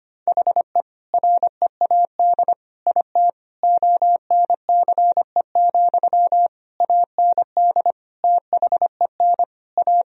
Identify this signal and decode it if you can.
Morse code, heard as a keyed tone, about 25 wpm, 700 Hz. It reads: HE READ IT ONCE, AND THEN A
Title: Sentences from Top 1000 Words - Review Part 1 - 25wpm